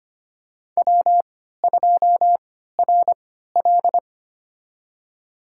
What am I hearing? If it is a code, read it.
Morse code, heard as a keyed tone, about 25 wpm, spaced as for 14 wpm, 700 Hz. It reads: W2RL
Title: USA Call Signs - 4 Characters - 3x Character Spacing - 25wpm